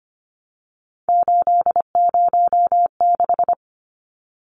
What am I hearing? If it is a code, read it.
Morse code, heard as a keyed tone, about 25 wpm, 700 Hz. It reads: 806
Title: Numbers - 3-Digit 25wpm